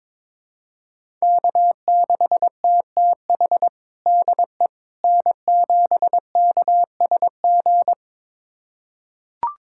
Morse code, heard as a keyed tone, about 21 wpm, 700 Hz. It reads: K6TTH DE N7KSG
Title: USA Callsign DE Callsign - 5 Characters - 22wpm